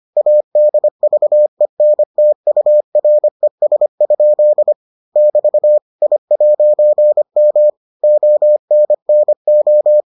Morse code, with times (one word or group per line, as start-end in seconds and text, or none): 0.2-4.7 ADVENTURES?
5.2-5.8 <BT>
6.0-7.7 I'M
8.0-10.0 ONNO